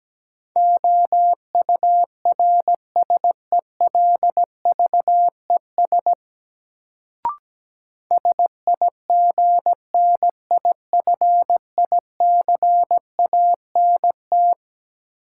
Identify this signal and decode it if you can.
Morse code, heard as a keyed tone, about 17 wpm, 700 Hz. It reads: OURSELVES SIGNIFICANT
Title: Sentences from Top 1600 Words - New Word Drill Alt - 17wpm